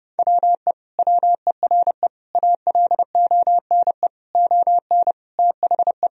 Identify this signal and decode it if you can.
Morse code, heard as a keyed tone, about 32 wpm, 700 Hz. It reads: WE WERE ALONE ON THE
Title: Sentences from Top 1000 Words - Review Part 3 - 30wpm